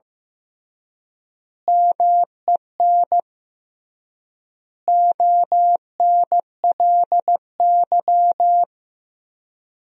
Morse code, heard as a keyed tone, about 15 wpm, 700 Hz. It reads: MEN ONLY
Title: 3x Word-Spacing - Sentences from Top 500 Words - 15wpm